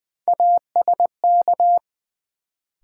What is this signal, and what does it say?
Morse code, heard as a keyed tone, about 20 wpm, 700 Hz. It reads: ASK
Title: Sentences from Top 200 Words - New Word Drill - 20wpm